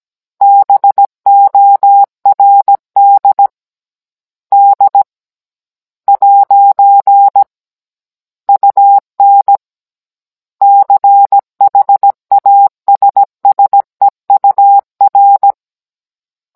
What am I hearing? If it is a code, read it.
Morse code, heard as a keyed tone, about 17 wpm, 800 Hz. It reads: BORD D ' UN CHASSEUR
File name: Cours-NR18-16mm